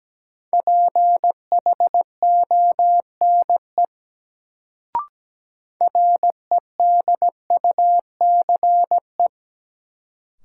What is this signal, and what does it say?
Morse code, heard as a keyed tone, about 17 wpm, 700 Hz. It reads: PHONE REDUCE
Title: Sentences from Top 1100 Words - New Word Drill Alt - 17wpm